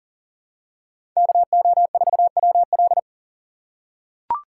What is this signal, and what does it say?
Morse code, heard as a keyed tone, about 38 wpm, 700 Hz. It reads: KO4WL
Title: USA Call Signs - 5 Characters - 4x Character Spacing - 40wpm